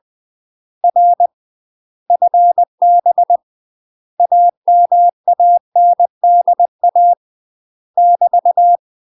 Morse code, heard as a keyed tone, about 20 wpm, 700 Hz. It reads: R FB AMANDA <BT>
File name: UK MASTER QSOs TRAINING / 20WPM / UK MASTER QSOs 0053